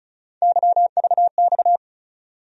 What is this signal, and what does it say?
Morse code, heard as a keyed tone, about 36 wpm, 700 Hz. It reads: YVX
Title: USA Call Sign Postfixes - 3 Letters 35wpm